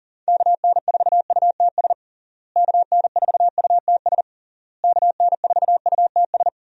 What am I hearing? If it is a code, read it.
Morse code, heard as a keyed tone, about 40 wpm, 700 Hz. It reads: KN4UTS KN4UTS KN4UTS
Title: USA Call Signs - 6 Characters - Repeated 3x - 40wpm